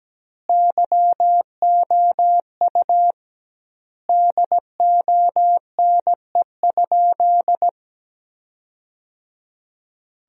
Morse code, heard as a keyed tone, about 17 wpm, 700 Hz. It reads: YOU DONE?
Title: 2x Word-Spacing - Sentences from Top 500 Words - 17wpm